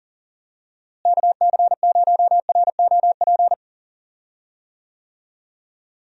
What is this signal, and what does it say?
Morse code, heard as a keyed tone, about 40 wpm, 700 Hz. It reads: KC0ROP
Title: USA Call Signs - 6 Characters - Repeated 2x - 40wpm